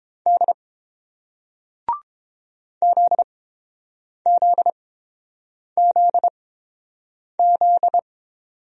Morse code, heard as a keyed tone, about 30 wpm, spaced as for 7 wpm, 700 Hz. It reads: D ZZZZ